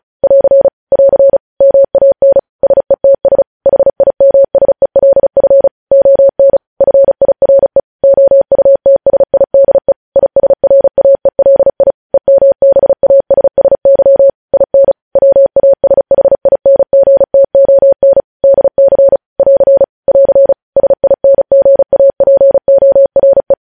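Morse code, heard as a keyed tone, about 36 wpm, 550 Hz. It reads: <AR> <AR> MAN SETS HIMSELF ON FIRE OUTSIDE ISRAELI EMBASSY IN WASHINGTON DC <AR> <AR> SINGAPORE
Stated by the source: Morse practice sending news headlines at 35 wpm.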